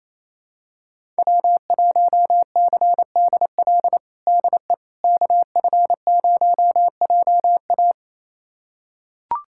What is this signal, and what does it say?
Morse code, heard as a keyed tone, about 28 wpm, 700 Hz. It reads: W1CDL DE KF0JA
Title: USA Callsign DE Callsign - 5 Characters - 28wpm